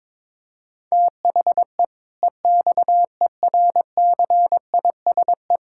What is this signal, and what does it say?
Morse code, heard as a keyed tone, about 22 wpm, 700 Hz. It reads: THE EXERCISE
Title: Sentences from Top 1000 Words - Review Part 6 - 22wpm